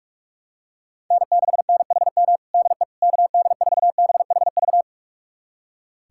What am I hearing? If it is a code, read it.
Morse code, heard as a keyed tone, about 45 wpm, 700 Hz. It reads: N6NHM DE KD4BHV
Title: USA Callsign DE Callsign 45wpm